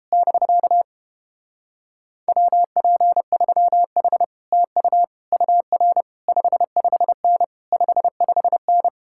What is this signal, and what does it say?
Morse code, heard as a keyed tone, about 31 wpm, 700 Hz. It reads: <BK> WP3H TU UR 55N 55N